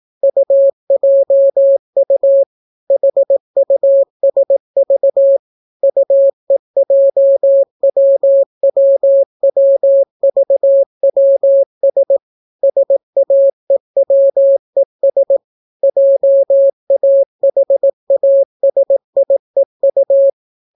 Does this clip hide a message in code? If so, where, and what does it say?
Yes: Morse code from 0.2 to 20.3 s, heard as a keyed tone, about 18 wpm, 550 Hz. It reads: UJU HUSV UEJWWWVWS SAEWES JAHASIEU